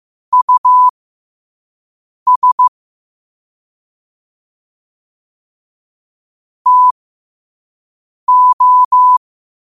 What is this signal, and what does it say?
Morse code, heard as a keyed tone, about 15 wpm, spaced as for 5 wpm, 1000 Hz. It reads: US TO
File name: 5wpm